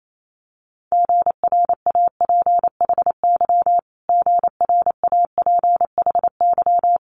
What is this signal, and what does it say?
Morse code, heard as a keyed tone, about 28 wpm, 700 Hz. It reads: GRAPHY GRAPHY
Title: Multi-letter Phonemes Repeated 3x 28wpm